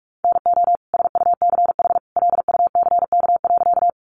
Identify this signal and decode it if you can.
Morse code, heard as a keyed tone, about 44 wpm, 700 Hz. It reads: NO SUCH LUCK.
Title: Sentences from Top 1500 Words 45wpm